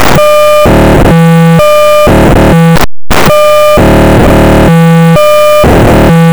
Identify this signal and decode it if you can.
POCSAG pager data.